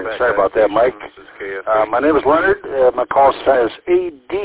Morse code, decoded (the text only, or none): none